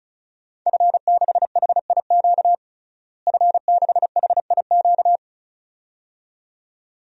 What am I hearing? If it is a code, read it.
Morse code, heard as a keyed tone, about 36 wpm, 700 Hz. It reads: F6HIQ F6HIQ
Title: DXCC-100 Call Signs - 5 Characters - Repeated 2x - 35wpm